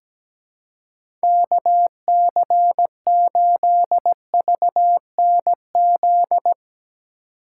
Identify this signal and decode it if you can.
Morse code, heard as a keyed tone, about 17 wpm, 700 Hz. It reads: KC8VNZ
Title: USA Call Signs - 6 Characters - 4x Character Spacing - 17wpm